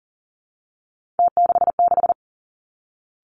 Morse code, heard as a keyed tone, about 40 wpm, 690 Hz. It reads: T66
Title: Three Challenging Letter-Number Combinations 40wpm